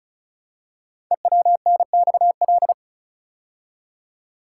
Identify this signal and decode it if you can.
Morse code, heard as a keyed tone, about 36 wpm, 690 Hz. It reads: EWNXL
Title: Code Groups - 5 Letters - 35wpm